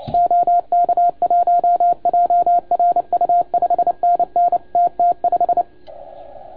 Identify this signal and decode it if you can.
Morse code, heard as a keyed tone, about 29 wpm, 690 Hz. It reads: OK1JRU 5NN TT5
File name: Archiv / CUC287 / OK1FGD CUC 287